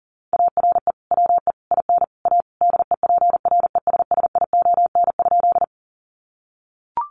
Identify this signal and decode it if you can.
Morse code, heard as a keyed tone, about 40 wpm, 700 Hz. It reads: ARE WE IN A DEPRESSION?
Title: Sentences from Top 1300 Words 40wpm